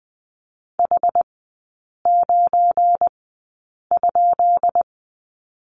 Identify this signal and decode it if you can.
Morse code, heard as a keyed tone, about 20 wpm, spaced as for 8 wpm, 700 Hz. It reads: H9?
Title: 172 Lesson 28e - Number 9 - Three Characters 20wpm